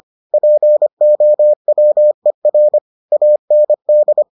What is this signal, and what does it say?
Morse code, heard as a keyed tone, about 25 wpm, 600 Hz. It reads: POWER AND